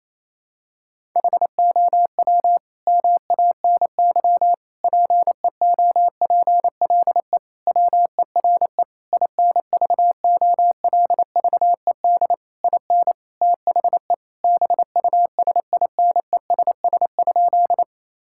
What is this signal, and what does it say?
Morse code, heard as a keyed tone, about 28 wpm, 700 Hz. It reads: HOW MANY PEOPLE WERE INVOLVED IN THE BUSINESS?
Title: Sentences from Top 1000 Words - Review Part 5 - 28wpm